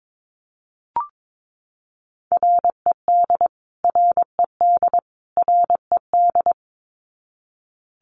Morse code, heard as a keyed tone, about 22 wpm, 700 Hz. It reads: RED RED RED
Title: Top 500 Words Repeated 22wpm